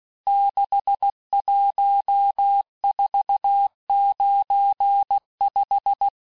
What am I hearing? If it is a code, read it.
Morse code, heard as a keyed tone, about 16 wpm, 800 Hz. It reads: 61495
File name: cours-du-lundi files / Cours-1586-texte-chiffre-vitesse-15mpm